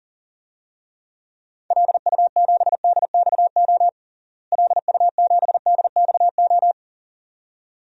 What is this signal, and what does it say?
Morse code, heard as a keyed tone, about 40 wpm, 700 Hz. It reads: LU7DXO LU7DXO
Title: DXCC-100 Call Signs - 6 Characters - Repeated 2x - 40wpm